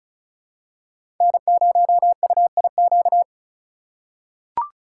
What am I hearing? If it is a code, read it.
Morse code, heard as a keyed tone, about 36 wpm, 700 Hz. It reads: N0UIQ
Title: USA Call Signs - 5 Characters - 4x Character Spacing - 35wpm